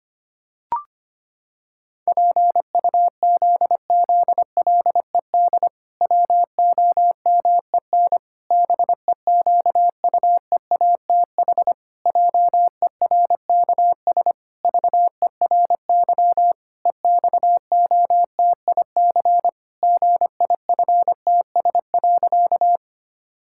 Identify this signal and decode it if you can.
Morse code, heard as a keyed tone, about 25 wpm, 700 Hz. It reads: PUZZLED WOMEN BEQUEATH JERKS VERY EXOTIC GIFTS.
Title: Pangram Challenge v2 - 25wpm